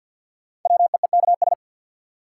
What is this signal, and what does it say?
Morse code, heard as a keyed tone, about 48 wpm, 700 Hz. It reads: WEEKS
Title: Sets of 5 Words - New Word Drill - 50wpm